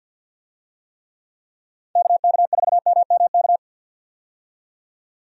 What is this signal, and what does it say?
Morse code, heard as a keyed tone, about 48 wpm, 700 Hz. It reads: KK4MMK
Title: USA Call Signs - 6 Characters - 50wpm